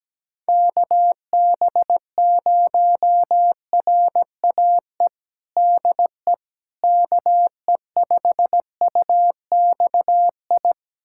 Morse code, heard as a keyed tone, about 17 wpm, 700 Hz. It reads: KB0RAE DE KE5UXI
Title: USA Callsign DE Callsign 17wpm